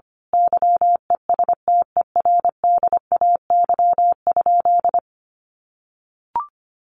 Morse code, heard as a keyed tone, about 25 wpm, 700 Hz. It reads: YESTERDAY?
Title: A Sound Of Thunder 25wpm